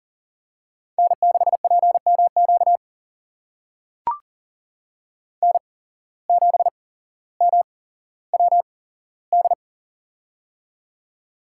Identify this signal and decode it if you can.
Morse code, heard as a keyed tone, about 40 wpm, spaced as for 17 wpm, 700 Hz. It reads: N6PMQ N7MWD
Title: USA Call Signs - 5 Characters - 8x Character Spacing - 40wpm